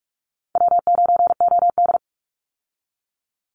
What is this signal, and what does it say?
Morse code, heard as a keyed tone, about 44 wpm, 700 Hz. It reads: W9OD
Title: Easy Call Signs Repeated 2x - 45wpm